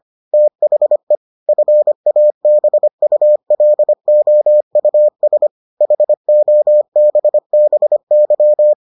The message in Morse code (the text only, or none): THE FABULOUS HOBBY